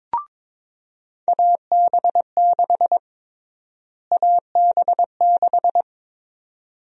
Morse code, heard as a keyed tone, about 22 wpm, 700 Hz. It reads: AB6 AB6
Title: USA Call Sign Prefixes - Repeated 2x - 22wpm